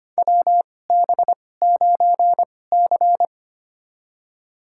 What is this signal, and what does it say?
Morse code, heard as a keyed tone, about 25 wpm, spaced as for 18 wpm, 700 Hz. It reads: WB9C
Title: USA Call Signs - 4 Characters - 2x Character Spacing - 25wpm